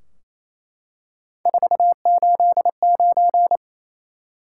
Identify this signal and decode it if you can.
Morse code, heard as a keyed tone, about 28 wpm, 700 Hz. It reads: 489
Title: Top 180 QSO Elements 28wpm